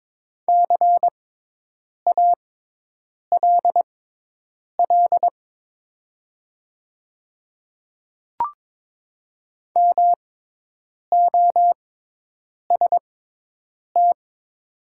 Morse code, heard as a keyed tone, about 22 wpm, spaced as for 8 wpm, 700 Hz. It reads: CALL MOST
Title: 6x Character Spacing - Top 100 Words - 22wpm